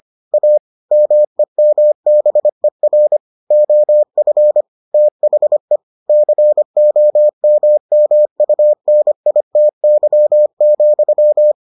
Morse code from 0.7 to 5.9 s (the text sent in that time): MEMBER OF THE